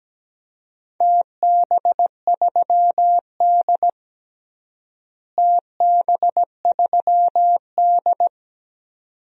Morse code, heard as a keyed tone, about 17 wpm, 700 Hz. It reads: TB3D TB3D
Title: DXCC-100 Call Signs - 4 Characters - Repeated 2x - 17wpm